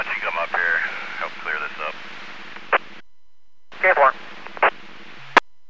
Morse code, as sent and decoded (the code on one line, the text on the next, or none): none